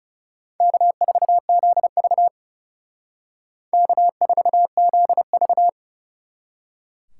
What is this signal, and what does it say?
Morse code, heard as a keyed tone, about 38 wpm, 700 Hz. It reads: K4ZV K4ZV